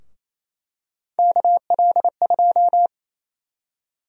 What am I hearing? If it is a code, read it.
Morse code, heard as a keyed tone, about 28 wpm, 700 Hz. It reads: KL2
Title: USA Call Sign Prefixes - 28wpm